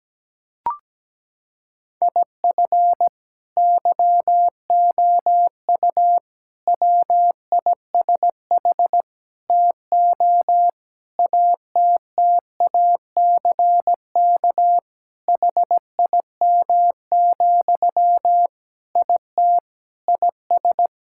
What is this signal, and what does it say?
Morse code, heard as a keyed tone, about 17 wpm, 700 Hz. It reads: IF YOU WISH TO ATTACK HIM, IT IS
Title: Sentences from Top 1600 Words 17wpm